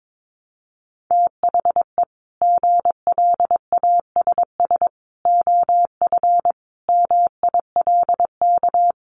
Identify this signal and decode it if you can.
Morse code, heard as a keyed tone, about 22 wpm, 700 Hz. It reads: THE GLASS OF MILK
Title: Additional Easy Phrases 22wpm